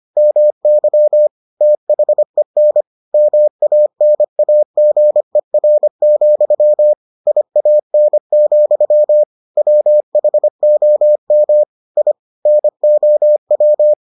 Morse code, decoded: MY THEN MANAGER, IAN, WHOM I NOW

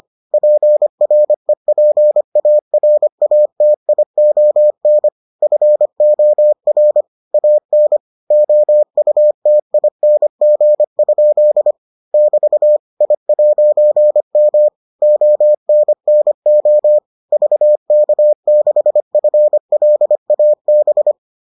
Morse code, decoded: PREPARATION FOR AN OUTING? <BT> I'M ONNO VK6FLAB

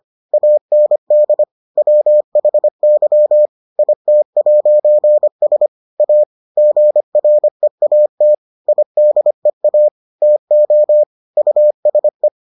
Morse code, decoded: AND WHY IT'S A GREAT IDEA TO USE

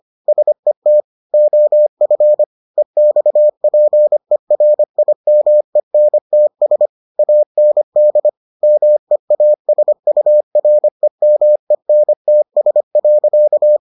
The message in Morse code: SET OF EXPERIMENTS AND MEASUREMENTS.

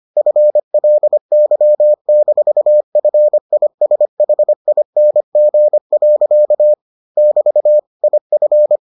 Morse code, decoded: FLY-FISHING. <BT> IF

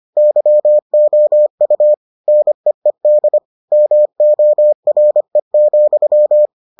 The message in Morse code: YOU NEED MORE,

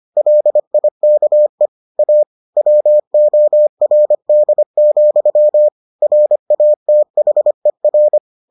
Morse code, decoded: LIKE A WORD, RATHER

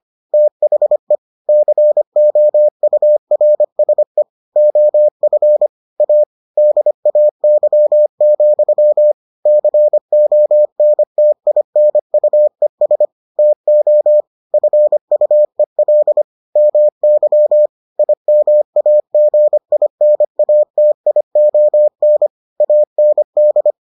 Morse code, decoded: THE COURSE OF A DAY, CONTINUES TO FUEL MY IMAGINATION AND

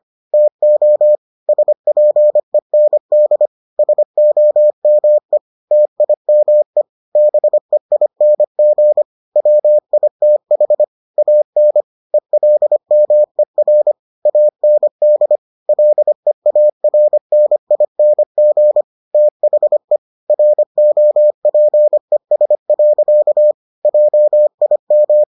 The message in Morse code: TO SPEND SOME TIME BEING WITH AN ELMER AND LEARNING THE ROPES. JIM

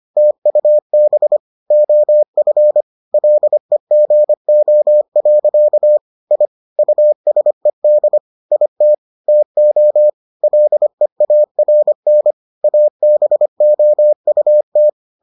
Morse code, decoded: TUB OF LEGO. I USED IT TO LEARN ABOUT

